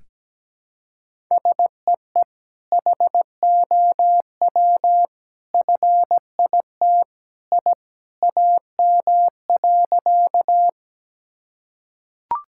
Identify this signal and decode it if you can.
Morse code, heard as a keyed tone, about 17 wpm, 700 Hz. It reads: SEE HOW FIT I AM.
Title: Sentences from Top 1000 Words - Review Part 5 - 17wpm